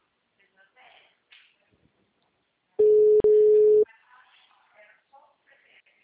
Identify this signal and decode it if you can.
Ringback tone (Europe, 425 Hz), with speech.